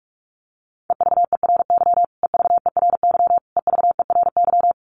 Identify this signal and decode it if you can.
Morse code, heard as a keyed tone, about 46 wpm, 700 Hz. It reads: EVERY EVERY EVERY
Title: Top 500 Words Repeated 45wpm